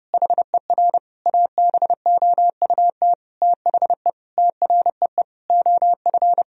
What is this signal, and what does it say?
Morse code, heard as a keyed tone, about 32 wpm, 700 Hz. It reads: HER ABOUT THE TREE OF
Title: Sentences from Top 1000 Words - Review Part 1 - 30wpm